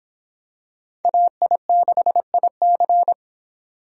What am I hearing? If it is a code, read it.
Morse code, heard as a keyed tone, about 26 wpm, 700 Hz. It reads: AI6IC